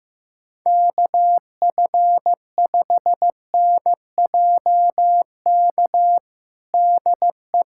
Morse code, heard as a keyed tone, about 15 wpm, 700 Hz. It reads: KF5NJK DE
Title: USA Callsign DE Callsign - 6 Characters - 15wpm